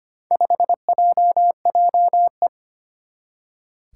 Morse code, heard as a keyed tone, about 25 wpm, 700 Hz. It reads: 5JJE
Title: Four Challenging Letter-Number Combinations 25wpm